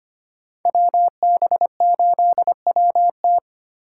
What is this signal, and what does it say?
Morse code, heard as a keyed tone, about 29 wpm, 700 Hz. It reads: WB8WT
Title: Speed Racing - USA 2x2 Call Signs 25wpm